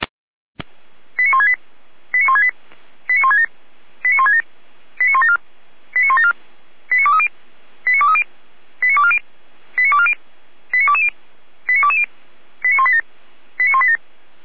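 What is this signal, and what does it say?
Five-tone selective call (ZVEI), heard as quick runs of five tones, tones of about 70 ms. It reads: 89168 89168 89167 89167 89174 89174 89230 89230 89240 89240 89220 89220 89187 89187
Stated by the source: ZVEI five-tone selective call received by SDR, served from audio.f85360.feuerwehr-orlamuende.org (Feuerwehr Orlamünde fire brigade).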